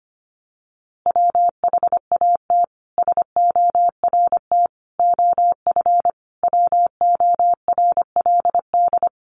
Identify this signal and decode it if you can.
Morse code, heard as a keyed tone, about 25 wpm, 700 Hz. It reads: WHAT SORT OF WORLD
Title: A Sound Of Thunder 25wpm